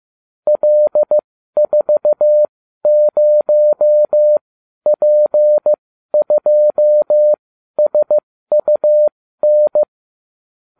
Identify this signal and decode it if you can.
Morse code, heard as a keyed tone, about 15 wpm, spaced as for 12 wpm, 600 Hz. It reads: L40P2SUN